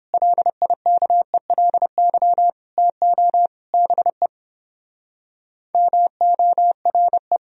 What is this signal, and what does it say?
Morse code, heard as a keyed tone, about 29 wpm, 700 Hz. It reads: LIKELY TO BE MORE